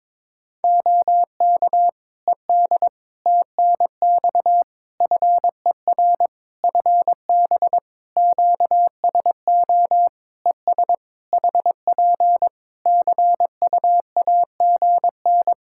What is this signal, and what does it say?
Morse code, heard as a keyed tone, about 22 wpm, 700 Hz. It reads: OK ED TNX FER FB QSO ES HP CUAGN
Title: K1USN SST QSO Practice 22wpm